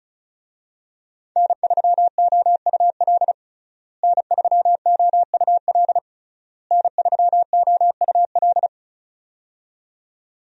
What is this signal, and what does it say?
Morse code, heard as a keyed tone, about 34 wpm, 700 Hz. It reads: N3OUL N3OUL N3OUL
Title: USA Call Signs - 5 Characters - Repeated 3x - 35wpm